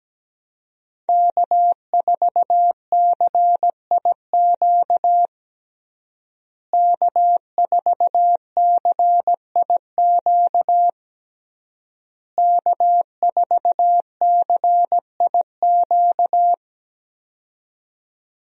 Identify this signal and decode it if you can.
Morse code, heard as a keyed tone, about 17 wpm, 700 Hz. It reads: K4CIQ K4CIQ K4CIQ
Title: USA Call Signs - 5 Characters - Repeated 3x - 17wpm